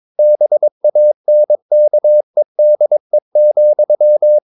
Morse code, heard as a keyed tone, about 22 wpm, 600 Hz. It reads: BANKEDE,
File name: Ferie QRS 22wpm